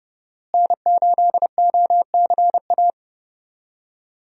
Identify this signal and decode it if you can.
Morse code, heard as a keyed tone, about 30 wpm, 700 Hz. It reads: N8OCA
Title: USA Call Signs - 5 Characters - Repeated 3x - 30wpm